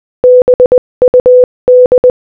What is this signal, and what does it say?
Morse code, heard as a keyed tone, about 20 wpm, 500 Hz. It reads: BUD